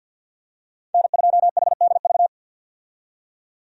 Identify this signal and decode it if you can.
Morse code, heard as a keyed tone, about 53 wpm, 700 Hz. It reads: N2HDV
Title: USA Call Signs - 5 Characters - 50wpm